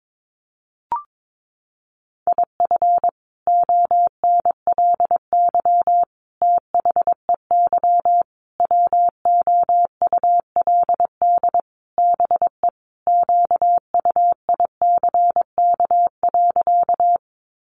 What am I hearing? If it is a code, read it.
Morse code, heard as a keyed tone, about 22 wpm, 700 Hz. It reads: IF ONLY THEY WOULD BE QUICK.
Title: Sentences from Top 500 Words 22wpm